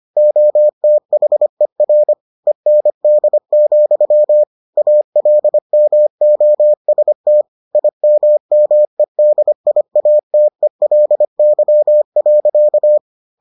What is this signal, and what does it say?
Morse code, heard as a keyed tone, about 25 wpm, 600 Hz. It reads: OTHER END, ALMOST IMMEDIATELY.